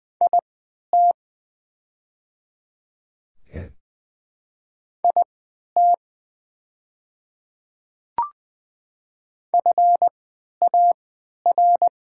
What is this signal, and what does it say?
Morse code, heard as a keyed tone, about 20 wpm, spaced as for 11 wpm, 700 Hz. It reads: IT IT FAR
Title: 3x Character Spacing - Top 300 Words - 20wpm